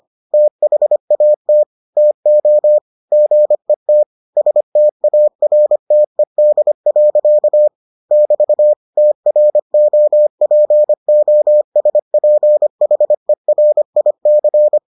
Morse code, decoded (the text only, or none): THAT TO GET STARTED. <BT> TROPOSPHERIC